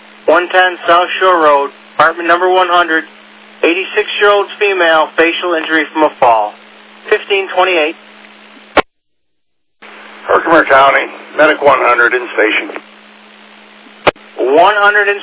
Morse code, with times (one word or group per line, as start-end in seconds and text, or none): none